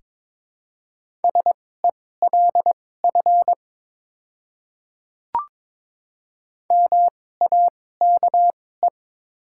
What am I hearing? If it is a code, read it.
Morse code, heard as a keyed tone, about 23 wpm, spaced as for 16 wpm, 700 Hz. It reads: SELF MAKE
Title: 2x Character Spacing - Top 200 Words - 22wpm